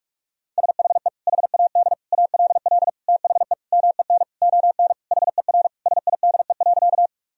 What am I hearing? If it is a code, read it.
Morse code, heard as a keyed tone, about 46 wpm, 700 Hz. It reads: SHE HAD ALL THE MEN ON HER SIDE.
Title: Sentences from Top 1000 Words - Review Part 1 - 45wpm